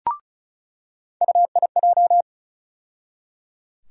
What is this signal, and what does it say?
Morse code, heard as a keyed tone, about 34 wpm, 700 Hz. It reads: UIJ
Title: USA Call Sign Postfixes - 3 Letters 35wpm